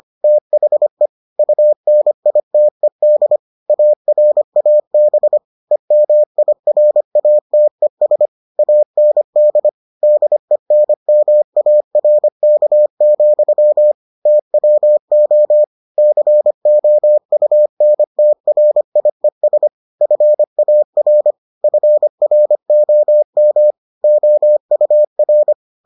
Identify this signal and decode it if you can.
Morse code, heard as a keyed tone, about 25 wpm, 600 Hz. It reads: THE UNITED ARAB EMIRATES AND DENMARK, TWO COUNTRIES FAR FROM OUR